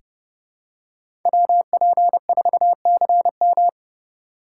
Morse code, heard as a keyed tone, about 32 wpm, 700 Hz. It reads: WP4CM
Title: USA Call Signs - 5 Characters - 2x Character Spacing - 30wpm